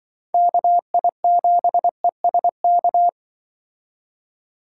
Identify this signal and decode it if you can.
Morse code, heard as a keyed tone, about 24 wpm, 700 Hz. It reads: KI7ESK
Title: Speed Racing - USA 2x3 Call Signs 20wpm